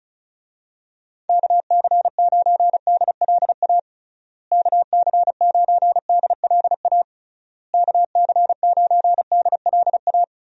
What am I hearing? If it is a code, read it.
Morse code, heard as a keyed tone, about 36 wpm, 700 Hz. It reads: KC9DLA KC9DLA KC9DLA
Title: USA Call Signs - 6 Characters - Repeated 3x - 35wpm